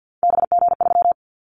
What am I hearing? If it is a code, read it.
Morse code, heard as a keyed tone, about 48 wpm, 700 Hz. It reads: BG3
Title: Callsign Prefixes for Top 100 DXCC - 50wpm